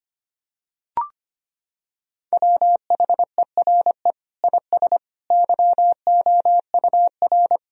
Morse code, heard as a keyed tone, about 25 wpm, 700 Hz. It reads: WHERE IS YOUR
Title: Sentences from Top 1000 Words - Review Part 1 - 25wpm